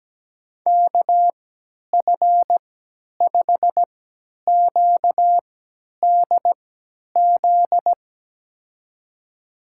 Morse code, heard as a keyed tone, about 17 wpm, spaced as for 10 wpm, 700 Hz. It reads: KF5QDZ
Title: USA Call Signs - 6 Characters - 3x Character Spacing - 17wpm